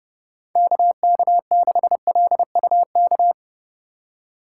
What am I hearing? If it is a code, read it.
Morse code, heard as a keyed tone, about 30 wpm, 700 Hz. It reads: KK6LUK
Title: USA Call Signs - 4x Character Spacing - 30wpm